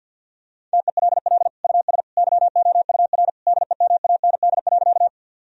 Morse code, heard as a keyed tone, about 50 wpm, 700 Hz. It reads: TELL US YOUR DEMAND.
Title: Sentences from Top 1000 Words - Review Part 4 - 50wpm